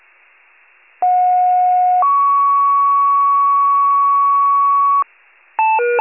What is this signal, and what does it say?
Two-tone page, heard as a one-second tone, then a three-second tone, about 730 then 1120 Hz. It is heard from 1.0 to 5.0 s.